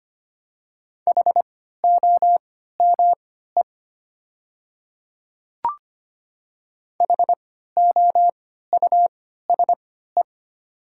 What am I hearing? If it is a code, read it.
Morse code, heard as a keyed tone, about 25 wpm, spaced as for 14 wpm, 700 Hz. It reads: HOME HOUSE
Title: 3x Character Spacing - Top 200 Words - 25wpm